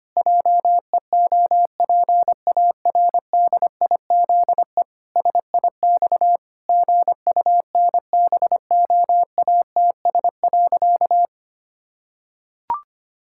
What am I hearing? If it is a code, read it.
Morse code, heard as a keyed tone, about 25 wpm, 700 Hz. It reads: JEOPARDIZE SIX GUNBOATS.
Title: Pangram Challenge v2 - 25wpm